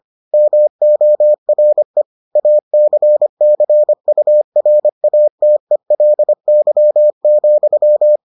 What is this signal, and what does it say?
Morse code, heard as a keyed tone, about 25 wpm, 600 Hz. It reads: MORE ACCURATELY,